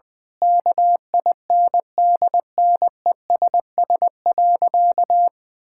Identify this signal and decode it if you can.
Morse code, heard as a keyed tone, about 20 wpm, 700 Hz. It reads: KINDNESS.
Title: Difficult Sentences with Hints - 20wpm